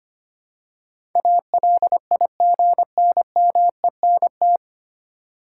Morse code, heard as a keyed tone, about 25 wpm, 700 Hz. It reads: ALIGNMENT
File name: practice / 25wpm / 9 Letter Words - First 3 Letters Given - 25wpm